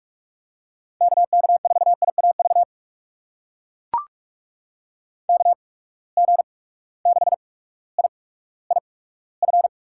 Morse code, heard as a keyed tone, about 43 wpm, spaced as for 19 wpm, 700 Hz. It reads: KK4IAV KG6IIF
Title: USA Call Signs - 6 Characters - 8x Character Spacing - 45wpm